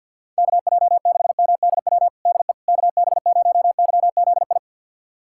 Morse code, heard as a keyed tone, about 50 wpm, 700 Hz. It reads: KJ6MDW DE KB0YZI
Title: USA Callsign DE Callsign 50wpm